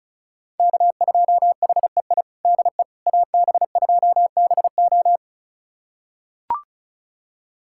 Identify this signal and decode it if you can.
Morse code, heard as a keyed tone, about 36 wpm, 700 Hz. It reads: K2HEI DE AB2BO
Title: USA Callsign DE Callsign - 5 Characters - 35wpm